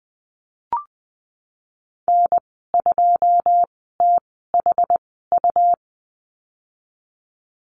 Morse code, heard as a keyed tone, about 20 wpm, spaced as for 15 wpm, 700 Hz. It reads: N2THU
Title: 128 Lesson 21g - Number 6 - Call Signs 20wpm